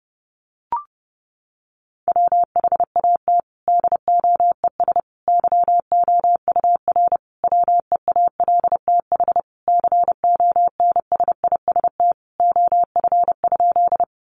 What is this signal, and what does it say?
Morse code, heard as a keyed tone, about 30 wpm, 700 Hz. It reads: WHAT DOES YOUR WEALTH CONSIST OF?
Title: Sentences from Top 1400 Words 30wpm